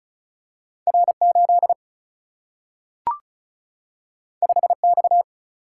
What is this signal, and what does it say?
Morse code, heard as a keyed tone, about 36 wpm, 700 Hz. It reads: R8 5X
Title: Single-Letter Single-Number 35wpm